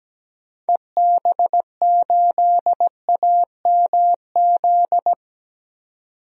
Morse code, heard as a keyed tone, about 17 wpm, 700 Hz. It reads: EB8AMZ
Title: DXCC-100 Call Signs - 6 Characters - 17wpm